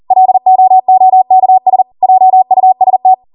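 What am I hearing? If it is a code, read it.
Morse code, heard as a keyed tone, about 40 wpm, 750 Hz. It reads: LOOKS JUST
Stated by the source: ARRL 40 WPM Morse code practice transmission.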